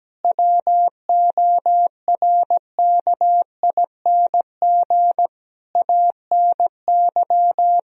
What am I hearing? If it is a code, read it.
Morse code, heard as a keyed tone, about 17 wpm, 700 Hz. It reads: WORKING ANY
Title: Sentences from Top 1000 Words - Review Part 4 - 17wpm